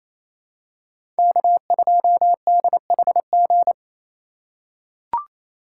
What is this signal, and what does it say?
Morse code, heard as a keyed tone, about 27 wpm, 700 Hz. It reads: K2DHG
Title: USA Call Signs - 5 Characters - 2x Character Spacing - 28wpm